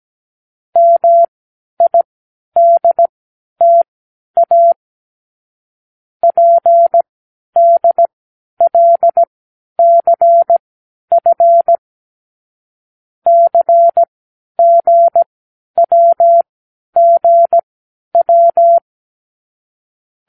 Morse code, heard as a keyed tone, about 17 wpm, spaced as for 10 wpm, 680 Hz. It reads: MIDTA PDLCF CGWGW
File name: k7qo / Lesson 36 (A-X)